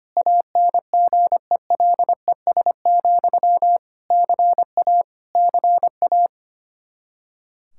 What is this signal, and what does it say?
Morse code, heard as a keyed tone, about 25 wpm, 700 Hz. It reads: ANGELES, CA CA
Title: K1USN SST QSO Practice 25wpm